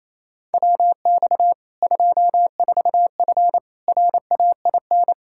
Morse code, heard as a keyed tone, about 28 wpm, 700 Hz. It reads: WX 24F RAIN
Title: Practice QSO - Exchange 2 - Info - 28wpm